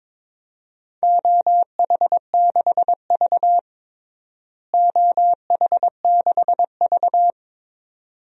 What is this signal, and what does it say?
Morse code, heard as a keyed tone, about 22 wpm, 700 Hz. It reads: OH6V OH6V
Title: DXCC-100 Call Signs - 4 Characters - Repeated 2x - 22wpm